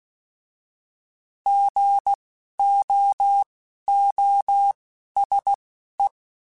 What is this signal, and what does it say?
Morse code, heard as a keyed tone, about 16 wpm, spaced as for 11 wpm, 800 Hz. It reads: GOOSE
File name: Alphabetical / Alphabetical mp3s / Gs @10